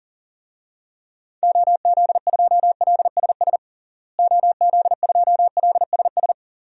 Morse code, heard as a keyed tone, about 40 wpm, 700 Hz. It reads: OZ2LSS OZ2LSS
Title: DXCC-100 Call Signs - 6 Characters - Repeated 3x - 40wpm